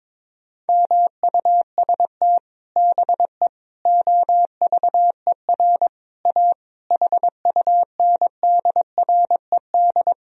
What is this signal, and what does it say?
Morse code, heard as a keyed tone, about 22 wpm, 700 Hz. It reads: MUST BE OVER A HUNDRED
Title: Sentences from Top 1000 Words - Review Part 2 - 22wpm